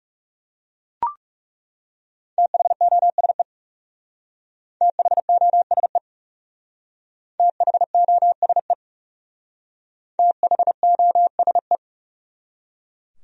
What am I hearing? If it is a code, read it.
Morse code, heard as a keyed tone, about 37 wpm, 700 Hz. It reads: THOSE THOSE THOSE THOSE
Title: Speed-Racing Top 300 Words 30wpm